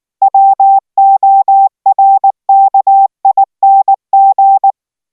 Morse code, heard as a keyed tone, about 19 wpm, 780 Hz. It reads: WORKING